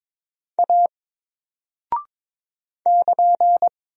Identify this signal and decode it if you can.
Morse code, heard as a keyed tone, about 22 wpm, 700 Hz. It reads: A <KN>
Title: Single Character (Letter:Number:Punctuation:Prosign) 22wpm